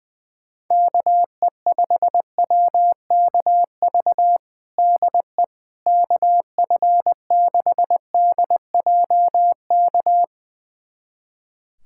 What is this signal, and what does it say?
Morse code, heard as a keyed tone, about 20 wpm, 700 Hz. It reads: KE5WKV DE KF6DJK
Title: USA Callsign DE Callsign - 6 Characters - 20wpm